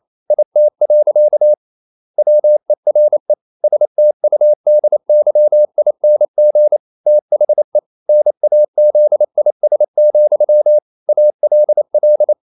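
Morse code, heard as a keyed tone, about 28 wpm, 600 Hz. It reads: IT. WERE STUDYING THE NAZIS, ALL
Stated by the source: Morse code practice at 28 wpm, text of Ken Follett's novel Never in English, from CW med Gnister.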